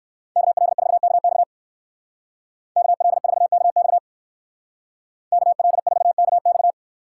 Morse code, heard as a keyed tone, about 59 wpm, 700 Hz. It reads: KL4KX KL4KX KL4KX